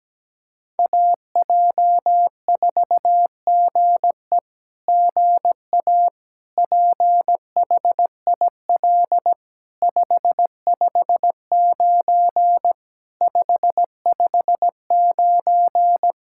Morse code, heard as a keyed tone, about 17 wpm, 700 Hz. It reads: AJ4GE GA PHIL 559 559
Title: SOTA QSO Practice 17wpm